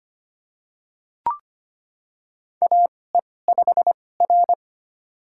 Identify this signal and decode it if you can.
Morse code, heard as a keyed tone, about 25 wpm, spaced as for 18 wpm, 700 Hz. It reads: AE5R
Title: USA Call Signs - 4 Characters - 2x Character Spacing - 25wpm